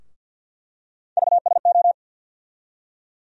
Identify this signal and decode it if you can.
Morse code, heard as a keyed tone, about 47 wpm, 700 Hz. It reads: VSO
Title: USA Call Sign Postfixes - 3 Letters 50wpm